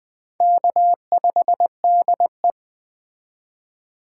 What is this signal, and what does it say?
Morse code, heard as a keyed tone, about 19 wpm, 700 Hz. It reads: K5DE